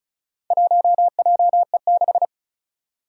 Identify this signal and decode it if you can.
Morse code, heard as a keyed tone, about 35 wpm, 700 Hz. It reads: 1JE6
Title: Four Challenging Letter-Number Combinations 35wpm